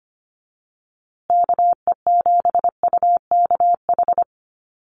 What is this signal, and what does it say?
Morse code, heard as a keyed tone, about 25 wpm, 700 Hz. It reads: KE7UKH
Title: USA Call Signs Repeated 3x 25wpm